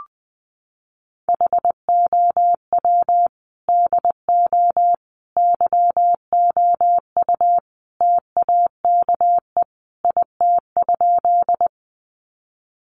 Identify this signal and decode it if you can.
Morse code, heard as a keyed tone, about 20 wpm, 700 Hz. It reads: HOW DO YOU TAKE IT?
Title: Sentences from Top 200 Words 20wpm